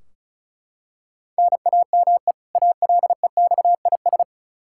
Morse code, heard as a keyed tone, about 36 wpm, 700 Hz. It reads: NAME ALEXIS
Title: Practice QSO - Exchange 1 - Beginning - 35wpm